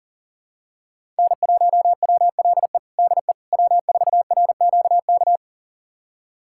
Morse code, heard as a keyed tone, about 40 wpm, 700 Hz. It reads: N1WLE DE W4RQK
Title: USA Callsign DE Callsign - 5 Characters - 40wpm